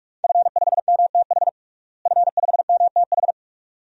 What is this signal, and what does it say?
Morse code, heard as a keyed tone, about 42 wpm, 700 Hz. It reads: F5MTH F5MTH
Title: DXCC-100 Call Signs - 5 Characters - Repeated 2x - 45wpm